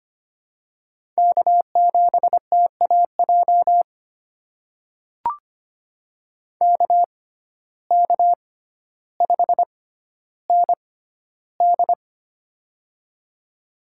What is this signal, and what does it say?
Morse code, heard as a keyed tone, about 25 wpm, spaced as for 13 wpm, 700 Hz. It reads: K7TAJ KK5ND
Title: USA Call Signs - 5 Characters - 6x Character Spacing - 25wpm